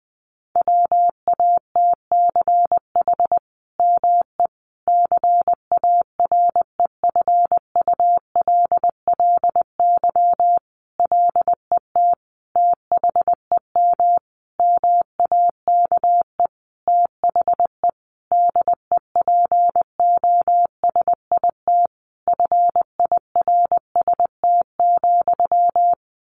Morse code, heard as a keyed tone, about 20 wpm, 700 Hz. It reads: WATCH ME CAREFULLY LET THEM MAKE THE DEPOSIT FIRST,
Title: Sentences from Top 1300 Words 20wpm